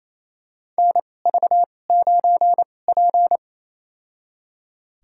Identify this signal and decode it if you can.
Morse code, heard as a keyed tone, about 28 wpm, spaced as for 20 wpm, 700 Hz. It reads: NV9P
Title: USA Call Signs - 4 Characters - 2x Character Spacing - 28wpm